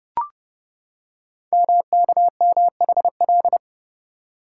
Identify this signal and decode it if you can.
Morse code, heard as a keyed tone, about 29 wpm, 700 Hz. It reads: MKMHL
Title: Code Groups - 5 Letters - 30wpm